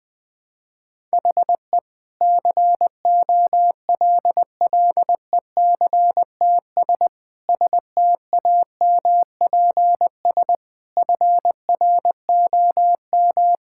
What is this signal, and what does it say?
Morse code, heard as a keyed tone, about 21 wpm, 700 Hz. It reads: HE COLLECTS STAMPS FROM
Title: Difficult Sentences with Hints - 20wpm